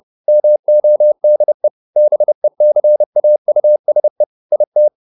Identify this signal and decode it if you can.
Morse code, heard as a keyed tone, about 30 wpm, 600 Hz. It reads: MODE BECAUSE IT